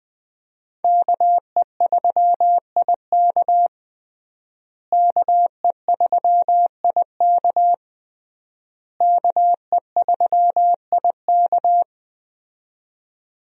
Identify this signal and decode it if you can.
Morse code, heard as a keyed tone, about 20 wpm, 700 Hz. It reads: KE3IK KE3IK KE3IK
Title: USA Call Signs - 5 Characters - Repeated 3x - 20wpm